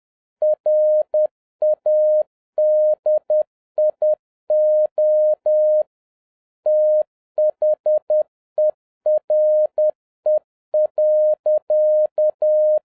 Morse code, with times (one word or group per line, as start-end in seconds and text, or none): 0.4-5.8 RADIO
6.7-12.8 THERE.